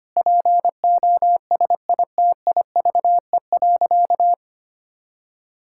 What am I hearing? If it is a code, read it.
Morse code, heard as a keyed tone, about 25 wpm, 700 Hz. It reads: POSITIVE.
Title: Sentences from Top 1000 Words - Review Part 5 - 25wpm